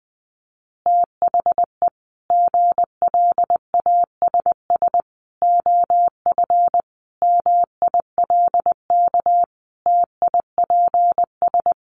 Morse code, heard as a keyed tone, about 20 wpm, 700 Hz. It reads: THE GLASS OF MILK TIPS
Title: Additional Easy Phrases 20wpm